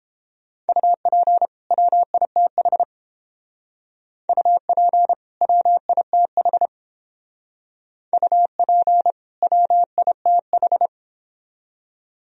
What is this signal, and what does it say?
Morse code, heard as a keyed tone, about 30 wpm, 700 Hz. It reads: UP WITH UP WITH UP WITH